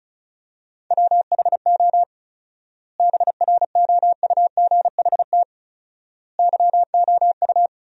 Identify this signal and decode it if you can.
Morse code, heard as a keyed tone, about 34 wpm, 700 Hz. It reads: WHO BROUGHT YOU
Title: 4x Word-Spacing - Sentences from Top 500 Words - 35wpm